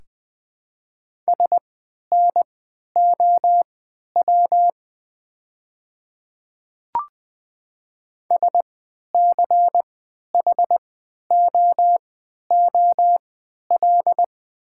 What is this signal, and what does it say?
Morse code, heard as a keyed tone, about 20 wpm, spaced as for 11 wpm, 700 Hz. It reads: SNOW SCHOOL
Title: 3x Character Spacing - Top 500 Words - 20wpm